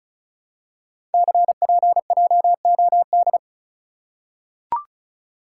Morse code, heard as a keyed tone, about 36 wpm, 700 Hz. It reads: CPJOD
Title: Code Groups - 5 Letters - 35wpm